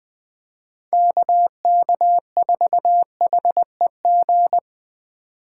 Morse code, heard as a keyed tone, about 20 wpm, 700 Hz. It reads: KK4HEG
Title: K1USN SST QSO Practice - 20wpm